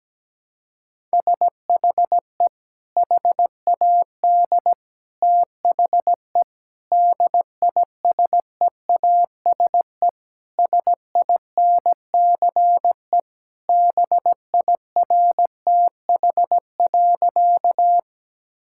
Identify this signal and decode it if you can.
Morse code, heard as a keyed tone, about 17 wpm, 700 Hz. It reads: SHE HAD THE DISEASE SINCE BIRTH.
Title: Sentences from Top 1000 Words - Review Part 6 - 17wpm